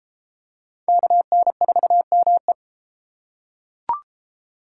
Morse code, heard as a keyed tone, about 33 wpm, 700 Hz. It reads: KN4ME